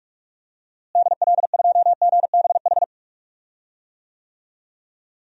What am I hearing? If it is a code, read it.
Morse code, heard as a keyed tone, about 46 wpm, 700 Hz. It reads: DL2GBH
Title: DXCC-100 Call Signs - 6 Characters - Repeated 3x - 45wpm